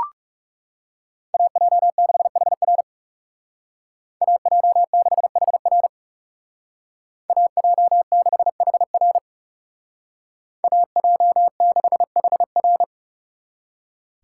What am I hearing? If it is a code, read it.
Morse code, heard as a keyed tone, about 39 wpm, 700 Hz. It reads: AJ6HR AJ6HR AJ6HR AJ6HR